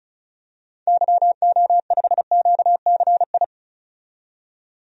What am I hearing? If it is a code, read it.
Morse code, heard as a keyed tone, about 36 wpm, 700 Hz. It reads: YO5QCI
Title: DXCC-100 Call Signs - 6 Characters - 35wpm